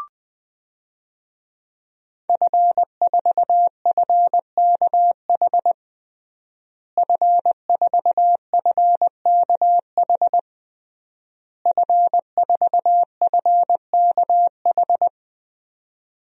Morse code, heard as a keyed tone, about 20 wpm, 700 Hz. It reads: F4FKH F4FKH F4FKH
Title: DXCC-100 Call Signs - 5 Characters - Repeated 3x - 20wpm